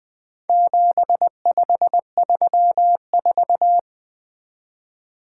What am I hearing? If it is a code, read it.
Morse code, heard as a keyed tone, about 20 wpm, 700 Hz. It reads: 7534
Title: Four Letter-Number Combinations - 20wpm